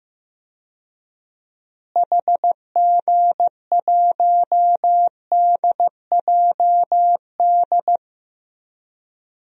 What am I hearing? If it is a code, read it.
Morse code, heard as a keyed tone, about 15 wpm, 700 Hz. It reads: HG1DJD
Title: DXCC-100 Call Signs - 6 Characters - Repeated 3x - 15wpm